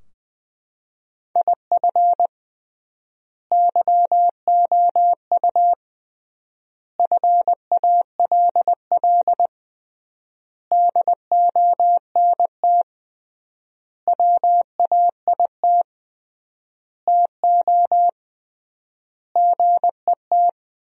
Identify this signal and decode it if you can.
Morse code, heard as a keyed tone, about 20 wpm, 700 Hz. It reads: IF YOU FALL DONT WAIT TO GET
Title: 3x Word-Spacing - Sentences from Top 500 Words - 20wpm